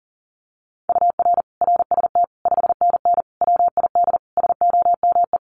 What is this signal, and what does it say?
Morse code, heard as a keyed tone, about 40 wpm, 700 Hz. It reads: UR RST 5NN WID SOME
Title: Common QSO Phrases 40wpm